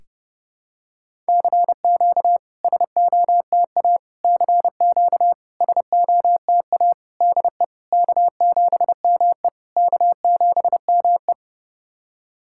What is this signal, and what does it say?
Morse code, heard as a keyed tone, about 30 wpm, 690 Hz. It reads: CQ SOTA CQ SOTA DE K7ME K7ME
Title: SOTA QSO Practice 30wpm